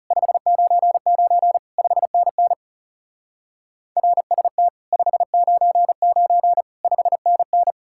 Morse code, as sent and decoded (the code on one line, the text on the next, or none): ..... ----. ----. / ..... -. -. / .-. ... - / ..... ----. ----. / ..... -. -.
599 5NN RST 599 5NN